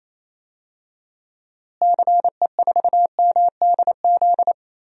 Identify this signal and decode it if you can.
Morse code, heard as a keyed tone, about 27 wpm, 700 Hz. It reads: CE4MDZ
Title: DXCC-100 Call Signs - 6 Characters - Repeated 3x - 28wpm